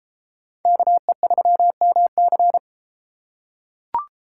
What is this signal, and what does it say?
Morse code, heard as a keyed tone, about 31 wpm, 700 Hz. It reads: KE3MC